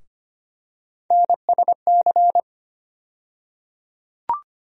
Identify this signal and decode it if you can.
Morse code, heard as a keyed tone, about 25 wpm, 700 Hz. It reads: NSC
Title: USA Call Sign Postfixes - 3 Letters 25wpm